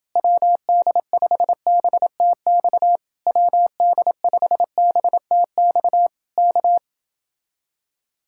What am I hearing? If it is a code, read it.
Morse code, heard as a keyed tone, about 30 wpm, 700 Hz. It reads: WD5BTX WD5BTX K